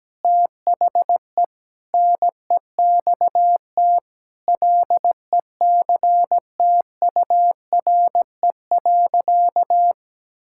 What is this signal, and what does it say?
Morse code, heard as a keyed tone, about 17 wpm, 700 Hz. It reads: THE NEXT LECTURE.
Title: Sentences from Top 1600 Words 17wpm